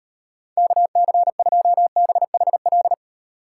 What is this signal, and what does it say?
Morse code, heard as a keyed tone, about 38 wpm, 700 Hz. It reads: KC2BHL